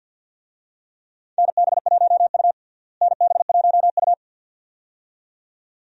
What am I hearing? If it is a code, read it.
Morse code, heard as a keyed tone, about 52 wpm, 700 Hz. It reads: NB1U NB1U
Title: USA Call Signs - 4 Characters - Repeated 2x - 50wpm